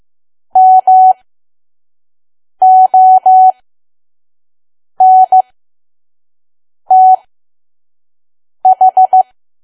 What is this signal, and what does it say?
Morse code, heard as a keyed tone, about 15 wpm, spaced as for 5 wpm, 750 Hz. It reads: MONTH
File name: Archive / 5 WPM files / PM